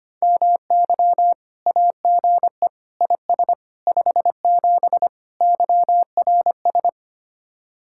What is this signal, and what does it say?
Morse code, heard as a keyed tone, about 31 wpm, 700 Hz. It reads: MY AGE IS 57 YRS
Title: Speed-Racing Common QSO Phrases - 25wpm